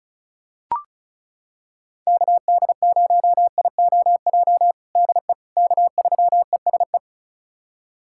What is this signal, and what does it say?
Morse code, heard as a keyed tone, about 36 wpm, 700 Hz. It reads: KD0IOJ DE K3ESE
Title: USA Callsign DE Callsign 35wpm